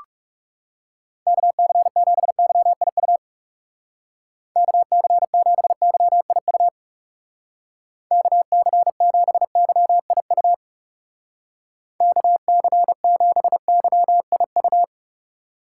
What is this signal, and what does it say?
Morse code, heard as a keyed tone, about 36 wpm, 700 Hz. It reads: KC7YIU KC7YIU KC7YIU KC7YIU
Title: Speed Racing - USA 2x3 Call Signs 30wpm